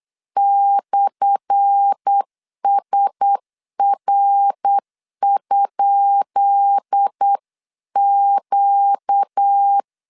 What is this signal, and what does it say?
Morse code, heard as a keyed tone, about 8 wpm, 800 Hz. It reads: /SR?Q
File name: LEZIONI / 10 SIGNS / 10B INT 8wpm 3534